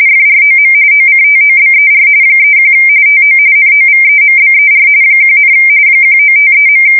RTTY, about 45.45 baud. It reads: FROM OCTOBER 31 TO NOVEMBER 1, 1953.
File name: fsk170 / teletype-RTTY